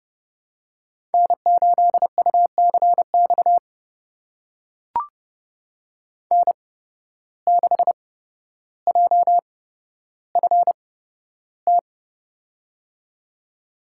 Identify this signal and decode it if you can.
Morse code, heard as a keyed tone, about 30 wpm, spaced as for 13 wpm, 700 Hz. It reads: N8UCX N6JFT
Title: USA Call Signs - 5 Characters - 8x Character Spacing - 30wpm